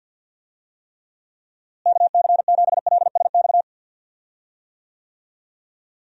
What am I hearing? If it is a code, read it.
Morse code, heard as a keyed tone, about 51 wpm, 700 Hz. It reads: KC7LSX
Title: USA Call Signs - 6 Characters - Repeated 3x - 50wpm